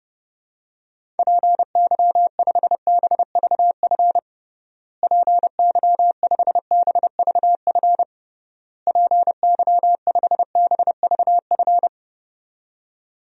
Morse code, heard as a keyed tone, about 30 wpm, 700 Hz. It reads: PY5BVF PY5BVF PY5BVF
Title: DXCC-100 Call Signs - 6 Characters - Repeated 3x - 30wpm